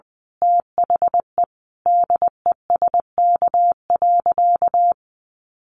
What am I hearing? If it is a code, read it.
Morse code, heard as a keyed tone, about 20 wpm, 700 Hz. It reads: THE DESK.
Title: A Sound Of Thunder 20wpm